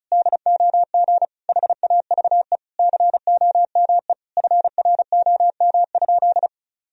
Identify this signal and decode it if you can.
Morse code, heard as a keyed tone, about 36 wpm, 700 Hz. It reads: DOG HAVE COME FROM?
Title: Sentences from Top 1000 Words - Review Part 2 - 35wpm